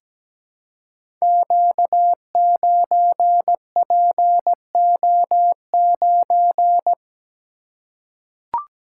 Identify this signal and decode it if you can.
Morse code, heard as a keyed tone, about 17 wpm, 690 Hz. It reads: Q9PO9
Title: Code Groups - 5 Letter:Numbers - 17wpm